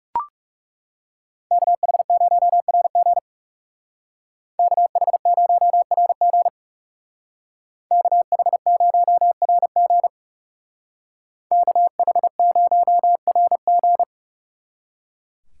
Morse code, heard as a keyed tone, about 39 wpm, 700 Hz. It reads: KH0RG KH0RG KH0RG KH0RG